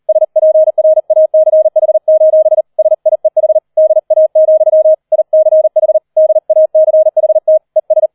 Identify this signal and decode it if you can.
Morse code, heard as a keyed tone, about 38 wpm, 630 Hz. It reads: SPRACH: SIEH DA, ICH DACHT ES